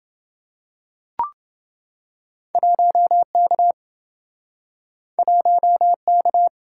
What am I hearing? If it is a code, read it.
Morse code, heard as a keyed tone, about 28 wpm, 700 Hz. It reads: 1K 1K